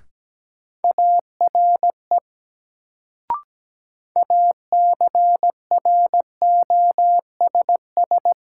Morse code, heard as a keyed tone, about 17 wpm, 700 Hz. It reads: ARE ACROSS
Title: Sets of 2 Words - New Word Drill - 17wpm